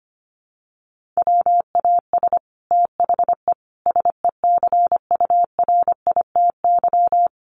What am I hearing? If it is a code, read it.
Morse code, heard as a keyed tone, about 25 wpm, 700 Hz. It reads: WAS THE SECURITY
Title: Sentences from Top 1400 Words 25wpm